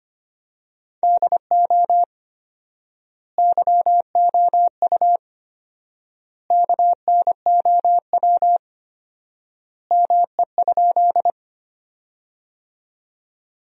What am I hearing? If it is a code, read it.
Morse code, heard as a keyed tone, about 25 wpm, 700 Hz. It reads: DO YOU KNOW ME?
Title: 4x Word-Spacing - Sentences from Top 200 Words - 25wpm